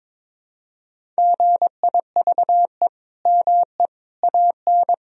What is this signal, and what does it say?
Morse code, heard as a keyed tone, about 22 wpm, 700 Hz. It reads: GIVE ME AN
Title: Sentences from Top 1000 Words - Review Part 1 - 22wpm